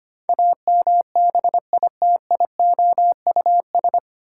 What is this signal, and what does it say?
Morse code, heard as a keyed tone, about 25 wpm, 700 Hz. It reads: AMBITIOUS
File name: Nine Letter Words - 25wpm